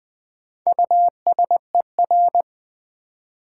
Morse code, heard as a keyed tone, about 20 wpm, 700 Hz. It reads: USER
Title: Build A Word - 1 Letter At A Time - 4-Letter Words - 20wpm